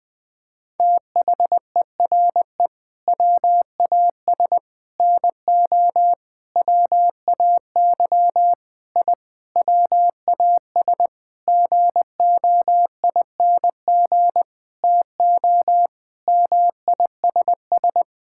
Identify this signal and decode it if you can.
Morse code, heard as a keyed tone, about 20 wpm, 700 Hz. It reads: THERE WAS NO WAY I WAS GOING TO MISS
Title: Sentences with 7-Word Phrases - Spoken Beginning - 20wpm